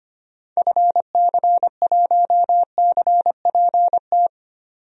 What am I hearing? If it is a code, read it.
Morse code, heard as a keyed tone, about 25 wpm, 700 Hz. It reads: FC1CPT
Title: DXCC-100 Call Signs - 6 Characters - 25wpm